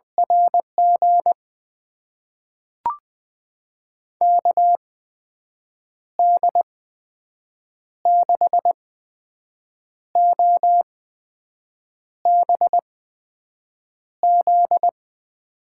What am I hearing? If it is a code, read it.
Morse code, heard as a keyed tone, about 20 wpm, spaced as for 6 wpm, 700 Hz. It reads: RG KD6OBZ
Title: USA Call Signs - 6 Characters - 8x Character Spacing - 20wpm